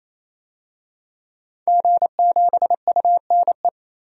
Morse code, heard as a keyed tone, about 28 wpm, 700 Hz. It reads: G7UNE
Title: DXCC-100 Call Signs - 5 Characters - 28wpm